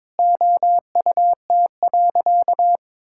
Morse code, heard as a keyed tone, about 22 wpm, 700 Hz. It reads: OUT.
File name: Sentences with 7-Word Phrases - 22wpm